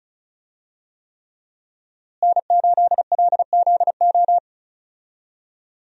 Morse code, heard as a keyed tone, about 36 wpm, 700 Hz. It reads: N8LZO
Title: USA Call Signs - 5 Characters - Repeated 3x - 35wpm